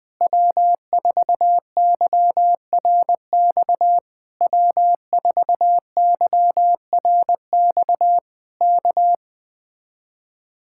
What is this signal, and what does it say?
Morse code, heard as a keyed tone, about 20 wpm, 700 Hz. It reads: W4YRX W4YRX K
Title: Speed-Racing Common QSO Phrases - 17wpm